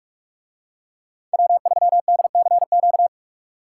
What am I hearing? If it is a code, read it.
Morse code, heard as a keyed tone, about 45 wpm, 700 Hz. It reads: W3DCQ